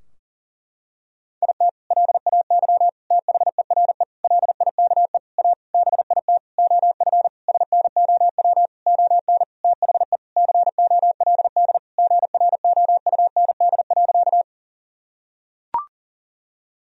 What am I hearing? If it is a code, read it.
Morse code, heard as a keyed tone, about 40 wpm, 700 Hz. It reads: IT LAY THERE LIKE A BIT OF SNOW ON THE COLD GROUND.
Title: Sentences from Top 1000 Words - Review Part 3 - 40wpm